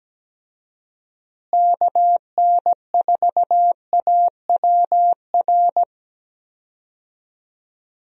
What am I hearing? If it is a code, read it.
Morse code, heard as a keyed tone, about 17 wpm, 700 Hz. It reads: KN4AWR
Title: USA Call Signs - 6 Characters - Repeated 2x - 17wpm